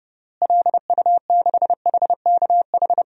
Morse code, heard as a keyed tone, about 29 wpm, 700 Hz. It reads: LU6HKH
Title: DXCC-100 Call Signs - 6 Characters - Repeated 3x - 30wpm